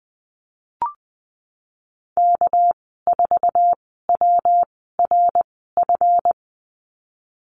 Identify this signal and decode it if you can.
Morse code, heard as a keyed tone, about 20 wpm, spaced as for 15 wpm, 700 Hz. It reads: K4WRF
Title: 200 Lesson 32g - Letter k - Call Signs 20wpm